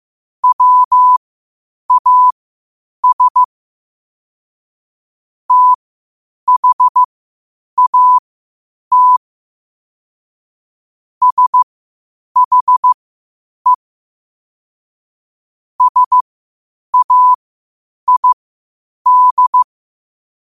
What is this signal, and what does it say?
Morse code, heard as a keyed tone, about 15 wpm, spaced as for 8 wpm, 1000 Hz. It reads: WAS THAT SHE SAID
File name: qotdfiles / 8wpm